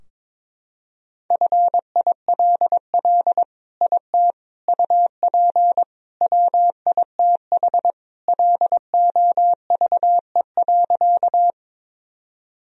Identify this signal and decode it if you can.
Morse code, heard as a keyed tone, about 21 wpm, 700 Hz. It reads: FILL IT UP WITH LOVE.
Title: Sentences from Top 1000 Words - Review Part 3 - 22wpm